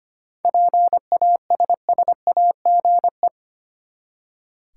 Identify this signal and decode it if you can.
Morse code, heard as a keyed tone, about 25 wpm, 700 Hz. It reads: PASSAGE
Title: Seven Letter Words - 25wpm